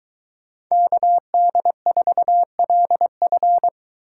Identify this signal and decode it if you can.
Morse code, heard as a keyed tone, about 23 wpm, 700 Hz. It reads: KD4LF